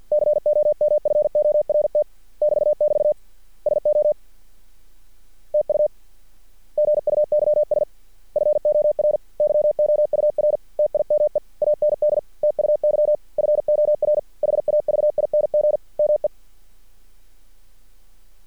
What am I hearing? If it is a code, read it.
Morse code, heard as a keyed tone, about 50 wpm, 600 Hz. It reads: COMFORT <BT><BT> SO TU GUYS FOR YOUR TIME AND TUY FOR HAVING ME
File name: archived / CRSnet